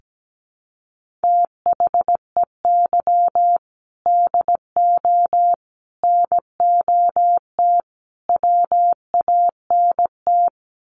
Morse code, heard as a keyed tone, about 17 wpm, 700 Hz. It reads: THEY DO NOT WANT
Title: Sets of 4 Words 17wpm